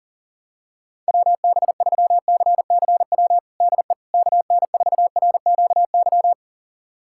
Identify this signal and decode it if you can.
Morse code, heard as a keyed tone, about 40 wpm, 700 Hz. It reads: WB3CCW DE KN4RQY
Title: USA Callsign DE Callsign - 6 Characters - 40wpm